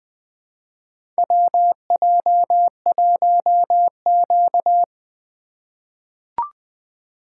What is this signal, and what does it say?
Morse code, heard as a keyed tone, about 20 wpm, 700 Hz. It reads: WJ1Q
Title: USA Call Signs - 4 Characters - 4x Character Spacing - 20wpm